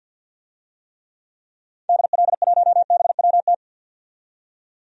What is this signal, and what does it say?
Morse code, heard as a keyed tone, about 52 wpm, 700 Hz. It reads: DL1BWT
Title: DXCC-100 Call Signs - 6 Characters - 50wpm